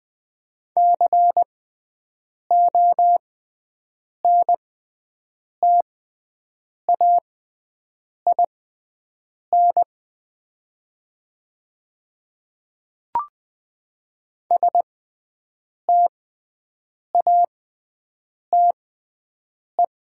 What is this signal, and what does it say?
Morse code, heard as a keyed tone, about 20 wpm, spaced as for 7 wpm, 700 Hz. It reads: CONTAIN STATE
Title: 6x Character Spacing - Top 500 Words - 20wpm